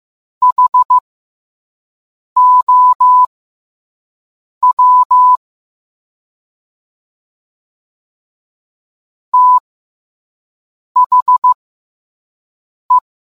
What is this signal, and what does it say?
Morse code, heard as a keyed tone, about 15 wpm, spaced as for 5 wpm, 1000 Hz. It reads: HOW THE